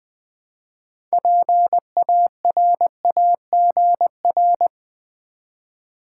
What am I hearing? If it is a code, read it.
Morse code, heard as a keyed tone, about 20 wpm, 700 Hz. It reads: PARAGR
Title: Build A Word - 3 Letters At A Time - 9-Letter Words - 20wpm